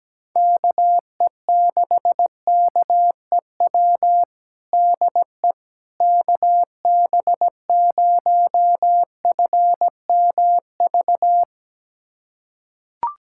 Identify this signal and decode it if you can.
Morse code, heard as a keyed tone, about 17 wpm, 700 Hz. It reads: KE6KEW DE KB0FMV
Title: USA Callsign DE Callsign 17wpm